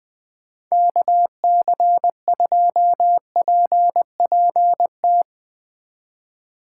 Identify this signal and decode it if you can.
Morse code, heard as a keyed tone, about 21 wpm, 700 Hz. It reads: KC2PPT
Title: USA Call Signs - 6 Characters - 6x Character Spacing - 20wpm